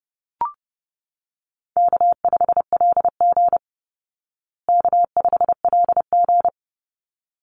Morse code, heard as a keyed tone, about 30 wpm, 700 Hz. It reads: K5LG K5LG
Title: Easy Call Signs Repeated 2x - 30wpm